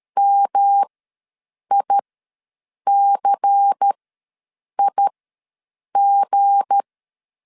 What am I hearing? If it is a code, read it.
Morse code, heard as a keyed tone, about 13 wpm, spaced as for 7 wpm, 800 Hz. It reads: MICIG
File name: LEZIONI / 8 YJ45 / 8 YJ45 ws 12wpm 3146